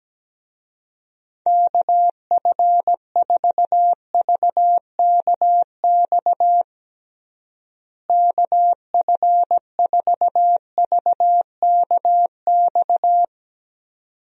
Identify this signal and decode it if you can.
Morse code, heard as a keyed tone, about 17 wpm, 700 Hz. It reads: KF4VKX KF4VKX
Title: USA Call Signs - 6 Characters - Repeated 2x - 17wpm